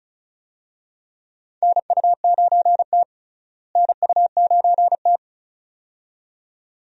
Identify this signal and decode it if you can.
Morse code, heard as a keyed tone, about 34 wpm, 700 Hz. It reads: NU9T NU9T
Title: USA Call Signs - 4 Characters - Repeated 2x - 35wpm